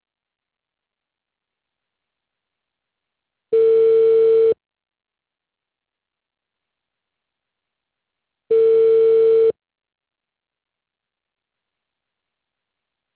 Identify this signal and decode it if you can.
Ringback tone (China, 450 Hz).